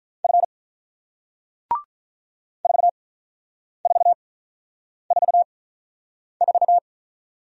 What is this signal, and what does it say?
Morse code, heard as a keyed tone, about 47 wpm, spaced as for 9 wpm, 700 Hz. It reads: F 4444